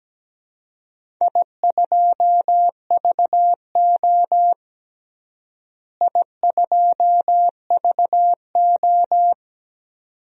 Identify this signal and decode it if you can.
Morse code, heard as a keyed tone, about 17 wpm, 700 Hz. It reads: I2VO I2VO
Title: DXCC-100 Call Signs - 4 Characters - Repeated 2x - 17wpm